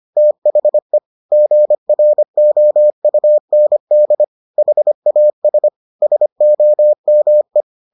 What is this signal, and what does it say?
Morse code, heard as a keyed tone, about 25 wpm, 600 Hz. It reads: THE GROUND HAS SOME